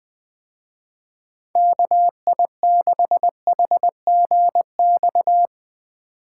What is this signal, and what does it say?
Morse code, heard as a keyed tone, about 20 wpm, 700 Hz. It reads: KI6HGX
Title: Speed Racing - USA Call Signs 17wpm